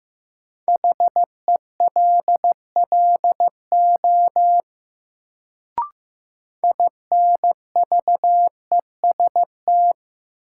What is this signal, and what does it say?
Morse code, heard as a keyed tone, about 16 wpm, 700 Hz. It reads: HELLO INVEST
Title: Sentences from Top 1100 Words - New Word Drill - 15wpm